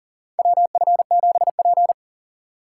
Morse code, heard as a keyed tone, about 40 wpm, 700 Hz. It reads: WF7P
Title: USA Call Signs - 4 Characters - 3x Character Spacing - 40wpm